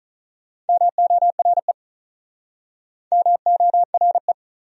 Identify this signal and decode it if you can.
Morse code, heard as a keyed tone, about 47 wpm, 700 Hz. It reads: MORE MORE